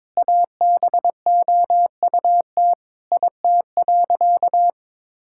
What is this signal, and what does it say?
Morse code, heard as a keyed tone, about 22 wpm, 700 Hz. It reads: ABOUT IT.
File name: practice / 22wpm / Sentences with 8-Word Phrases - 22wpm